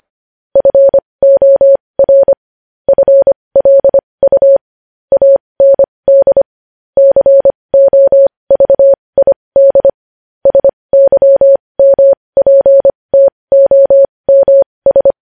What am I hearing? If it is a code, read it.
Morse code, heard as a keyed tone, about 25 wpm, spaced as for 20 wpm, 550 Hz. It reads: FOR FLU AND COVID SYMPTOMS
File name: podcast / News headlines-1x-25x20